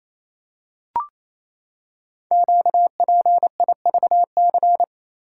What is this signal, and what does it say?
Morse code, heard as a keyed tone, about 27 wpm, 700 Hz. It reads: QPIVC
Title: Code Groups - 5 Letters - 28wpm